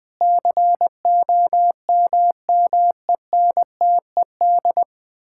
Morse code, heard as a keyed tone, about 20 wpm, 700 Hz. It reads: COMMENTED
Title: Sentences with 7-Word Phrases - 20wpm